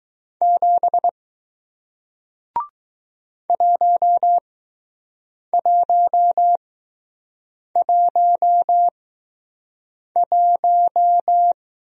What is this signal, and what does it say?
Morse code, heard as a keyed tone, about 20 wpm, spaced as for 6 wpm, 700 Hz. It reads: 7 1111